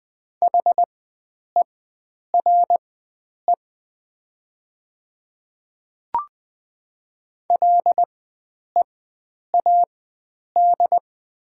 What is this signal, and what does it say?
Morse code, heard as a keyed tone, about 20 wpm, spaced as for 9 wpm, 700 Hz. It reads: HERE LEAD
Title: 4x Character Spacing - Top 500 Words - 20wpm